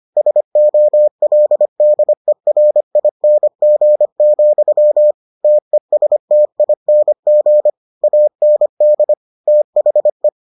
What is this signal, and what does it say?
Morse code, heard as a keyed tone, about 25 wpm, 600 Hz. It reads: SOLDERING, TESTING AND THE